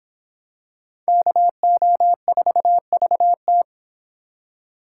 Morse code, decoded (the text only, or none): KO4VT